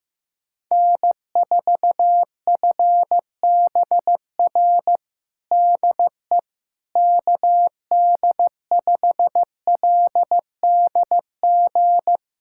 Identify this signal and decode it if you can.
Morse code, heard as a keyed tone, about 16 wpm, 700 Hz. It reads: N4FBR DE KD5LDG
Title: USA Callsign DE Callsign 15wpm